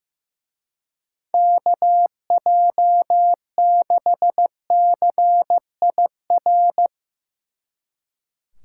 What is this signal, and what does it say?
Morse code, heard as a keyed tone, about 16 wpm, 700 Hz. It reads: KJ6CIR
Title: Speed Racing - USA 2x3 Call Signs 15wpm